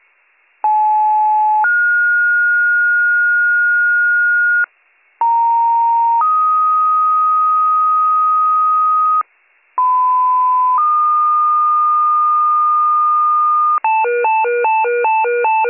Two-tone page, heard as a one-second tone, then a three-second tone, about 860 then 1430 Hz, about 930 then 1250 Hz, about 1010 then 1220 Hz, with noise. It is heard from 0.6 to 13.8 s.